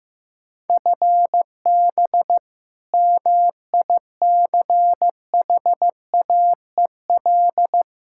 Morse code, heard as a keyed tone, about 15 wpm, 700 Hz. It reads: FB MICHAEL
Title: Practice QSO - Exchange 3 - More Info - 15wpm